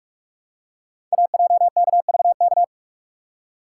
Morse code, heard as a keyed tone, about 46 wpm, 690 Hz. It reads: AJKVK
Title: Code Groups - 5 Letters - 45wpm